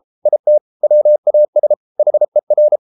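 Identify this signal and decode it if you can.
Morse code, heard as a keyed tone, about 33 wpm, 600 Hz. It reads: IT WAS HER
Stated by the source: Morse code practice at 32 wpm from CW med Gnister, reading Ken Follett's novel Never in English.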